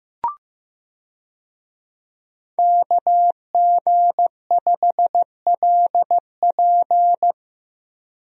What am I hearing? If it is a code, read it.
Morse code, heard as a keyed tone, about 15 wpm, 700 Hz. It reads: KG5LP
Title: USA Call Signs - 5 Characters - Repeated 3x - 15wpm